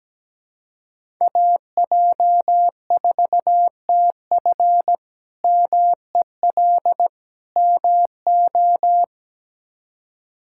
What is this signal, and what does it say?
Morse code, heard as a keyed tone, about 17 wpm, 700 Hz. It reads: AJ4TF MEL MO
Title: K1USN SST QSO Practice - 17wpm